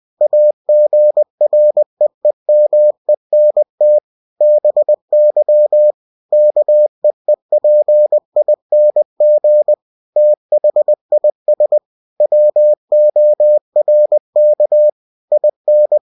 Morse code, heard as a keyed tone, about 20 wpm, 600 Hz. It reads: AGREEMENT BY KEEPING THIS WORK IN
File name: cw-books / winnie-the-pooh / winnie-20wpm-page-0114